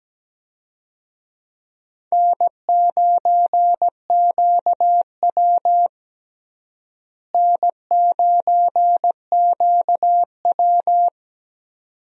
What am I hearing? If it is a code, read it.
Morse code, heard as a keyed tone, about 17 wpm, 700 Hz. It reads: N9QW N9QW
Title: USA Call Signs - 4 Characters - Repeated 2x - 17wpm